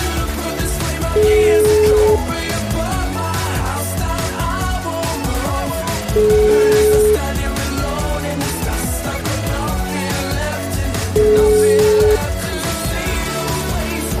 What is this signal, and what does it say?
Ringback tone (Europe, 425 Hz), with noise.